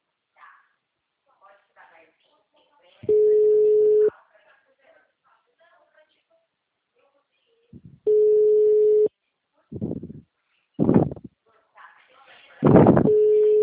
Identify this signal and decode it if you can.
Ringback tone (Europe, 425 Hz), with speech.
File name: out- -2004- - - .3225